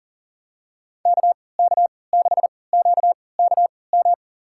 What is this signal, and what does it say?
Morse code, heard as a keyed tone, about 40 wpm, spaced as for 23 wpm, 700 Hz. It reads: KK6QKM
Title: USA Call Signs - 3x Character Spacing - 40wpm